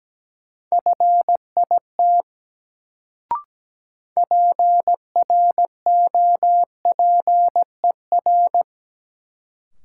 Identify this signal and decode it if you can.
Morse code, heard as a keyed tone, about 17 wpm, 700 Hz. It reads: FIT PROPER
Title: Sentences from Top 800 Words - New Word Drill - 17wpm